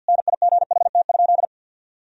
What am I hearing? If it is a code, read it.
Morse code, heard as a keyed tone, about 50 wpm, 700 Hz. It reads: NIGHT?
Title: Sentences from Top 1000 Words - Review Part 4 - 50wpm